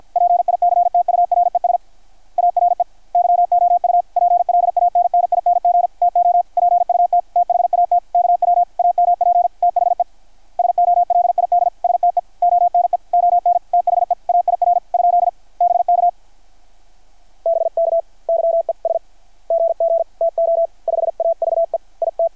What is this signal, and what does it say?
Morse code, heard as a keyed tone, about 51 wpm, 690 Hz. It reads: PICTURES ARE YOU PLANNING TO PUT THAT KW AMP THE SOLID STE ONE ON THE AIR ? BK BK YES GG TO 5AVE IT
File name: archived / CRSnet